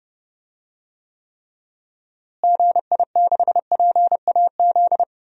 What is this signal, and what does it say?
Morse code, heard as a keyed tone, about 30 wpm, 700 Hz. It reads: GI6PAZ
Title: DXCC-100 Call Signs - 6 Characters - Repeated 3x - 30wpm